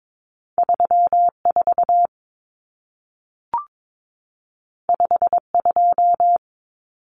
Morse code, heard as a keyed tone, about 22 wpm, 700 Hz. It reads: 34 52
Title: Numbers - 2-Digit 22wpm